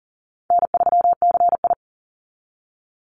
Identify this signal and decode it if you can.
Morse code, heard as a keyed tone, about 40 wpm, 700 Hz. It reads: N3CI
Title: Call Signs Repeated 2x - 40wpm